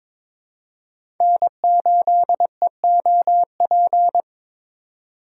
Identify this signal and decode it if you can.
Morse code, heard as a keyed tone, about 21 wpm, 700 Hz. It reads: N8EOP
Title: USA Call Signs - 2x Character Spacing - 22wpm